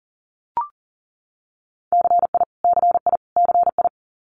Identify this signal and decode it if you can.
Morse code, heard as a keyed tone, about 38 wpm, 700 Hz. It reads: CI CI CI
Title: Multi-letter Phonemes Repeated 3x 40wpm